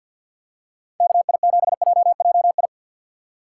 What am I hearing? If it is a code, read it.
Morse code, heard as a keyed tone, about 50 wpm, 700 Hz. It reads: KI7JJI
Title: USA Call Signs - 6 Characters - 6x Character Spacing - 50wpm